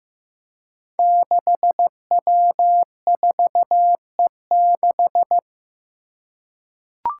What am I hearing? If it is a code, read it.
Morse code, heard as a keyed tone, about 15 wpm, 700 Hz. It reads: 6W4E6
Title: Five Challenging Letter-Number Combinations 15wpm